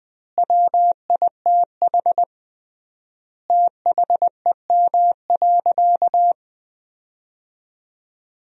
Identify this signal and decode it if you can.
Morse code, heard as a keyed tone, about 20 wpm, 700 Hz. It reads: WITH THEM.
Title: 3x Word-Spacing - Sentences from Top 200 Words - 20wpm